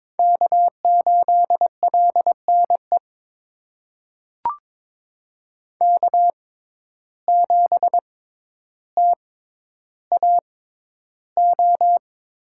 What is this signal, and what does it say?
Morse code, heard as a keyed tone, about 22 wpm, spaced as for 11 wpm, 700 Hz. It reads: K8LNE K7TAO
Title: USA Call Signs - 5 Characters - 6x Character Spacing - 22wpm